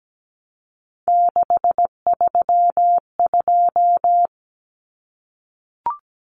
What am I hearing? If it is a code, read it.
Morse code, heard as a keyed tone, about 17 wpm, 700 Hz. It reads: 632
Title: Numbers - 3-Digit 17wpm